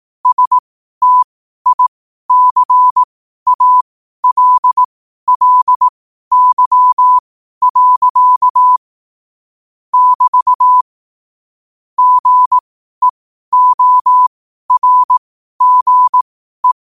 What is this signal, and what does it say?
Morse code, heard as a keyed tone, about 18 wpm, spaced as for 12 wpm, 1000 Hz. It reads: STICALLY. <BT> GEORGE
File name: qotdfiles / 12wpm